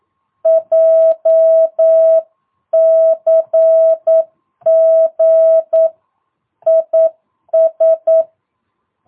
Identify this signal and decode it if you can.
Morse code, heard as a keyed tone, about 9 wpm, 650 Hz. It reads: JCG IS
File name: jh8cbh / 2022cw